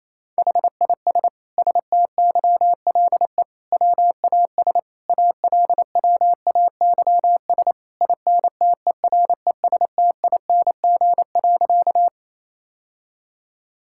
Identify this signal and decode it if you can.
Morse code, heard as a keyed tone, about 28 wpm, 700 Hz. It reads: HIS STYLE WAS ALWAYS INTERESTING.
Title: Sentences from Top 1000 Words - Review Part 5 - 28wpm